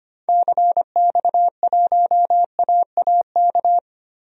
Morse code, heard as a keyed tone, about 25 wpm, 700 Hz. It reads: CX1AAK
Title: DXCC-100 Call Signs - 6 Characters - Repeated 2x - 25wpm